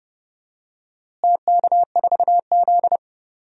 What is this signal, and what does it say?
Morse code, heard as a keyed tone, about 32 wpm, 700 Hz. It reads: TK4Z
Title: DXCC-100 Call Signs - 4 Characters - Repeated 2x - 30wpm